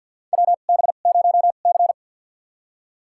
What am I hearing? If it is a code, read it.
Morse code, heard as a keyed tone, about 52 wpm, spaced as for 37 wpm, 690 Hz. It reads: WB0C
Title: USA Call Signs - 2x Character Spacing - 50wpm